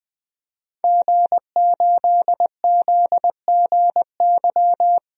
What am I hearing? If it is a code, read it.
Morse code, heard as a keyed tone, about 20 wpm, 700 Hz. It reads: G8ZGY
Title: DXCC-100 Call Signs - 5 Characters - 20wpm